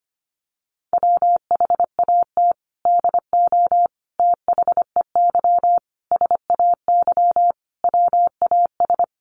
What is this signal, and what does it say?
Morse code, heard as a keyed tone, about 25 wpm, 700 Hz. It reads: WHAT DO THEY SAY WAS
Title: Sentences from Top 200 Words 25wpm